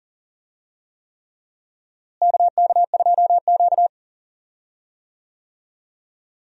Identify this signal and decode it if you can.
Morse code, heard as a keyed tone, about 40 wpm, 700 Hz. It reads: KK2Q
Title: USA Call Signs - 4 Characters - Repeated 3x - 40wpm